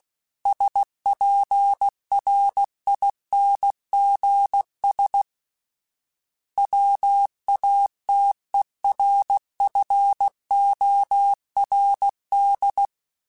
Morse code, heard as a keyed tone, about 16 wpm, 800 Hz. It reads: SPRINGS WATERFORD
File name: fromSMs / FromSMs mp3s / Cities@15